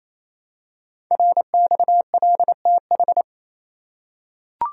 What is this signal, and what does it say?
Morse code, heard as a keyed tone, about 28 wpm, 700 Hz. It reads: RXLTH
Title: Code Groups - 5 Letters - 28wpm